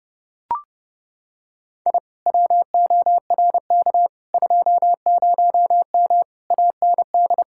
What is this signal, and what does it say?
Morse code, heard as a keyed tone, about 30 wpm, 700 Hz. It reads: I WORK 20M AND